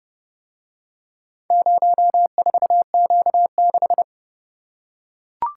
Morse code, heard as a keyed tone, about 28 wpm, 700 Hz. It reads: 04Q6